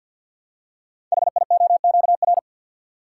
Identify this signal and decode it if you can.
Morse code, heard as a keyed tone, about 53 wpm, 700 Hz. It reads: HIOQR
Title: Code Groups - 5 Letter:Numbers - 50wpm